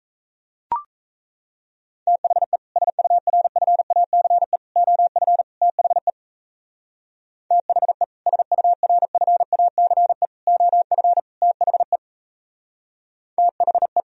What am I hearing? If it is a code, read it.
Morse code, heard as a keyed tone, about 35 wpm, 700 Hz. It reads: THE SURFACE OF THE THE SURFACE OF THE THE